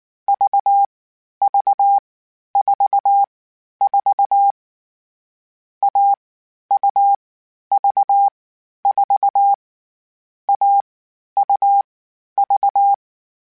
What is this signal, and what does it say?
Morse code, heard as a keyed tone, about 19 wpm, 800 Hz. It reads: V V 4 4 A U V 4 A U V
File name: Difficult Combos / Difficult Combos-mp3s / AUV AUV4 @18